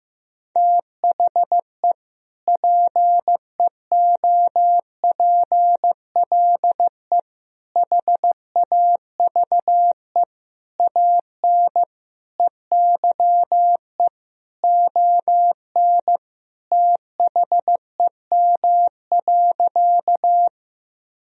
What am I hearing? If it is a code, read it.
Morse code, heard as a keyed tone, about 15 wpm, 700 Hz. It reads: THE PEOPLE HAVE AN EYE ON THEM.
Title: Sentences from Top 1000 Words - Review Part 1 - 15wpm